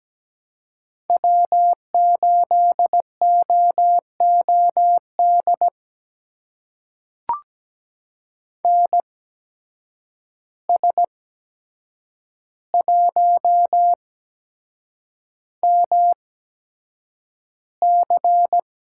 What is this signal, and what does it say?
Morse code, heard as a keyed tone, about 17 wpm, spaced as for 7 wpm, 700 Hz. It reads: W8OOD NS1MC
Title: USA Call Signs - 5 Characters - 8x Character Spacing - 17wpm